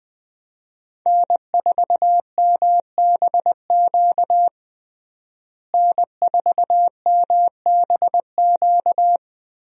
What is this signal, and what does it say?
Morse code, heard as a keyed tone, about 21 wpm, 700 Hz. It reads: N4MBQ N4MBQ
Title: USA Call Signs - 5 Characters - Repeated 3x - 20wpm